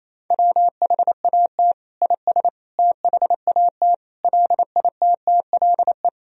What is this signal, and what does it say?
Morse code, heard as a keyed tone, about 28 wpm, 700 Hz. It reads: WHAT IS THAT LITTLE
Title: Sentences from Top 1000 Words - Review Part 1 - 28wpm